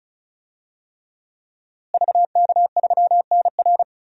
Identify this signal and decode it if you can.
Morse code, heard as a keyed tone, about 36 wpm, 700 Hz. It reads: VK3NR
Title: DXCC-100 Call Signs - 5 Characters - Repeated 3x - 35wpm